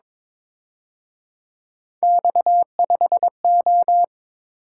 Morse code, heard as a keyed tone, about 26 wpm, 700 Hz. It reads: X5O
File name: Speed-Racing - Three Letter-Number Combinations - 22wpm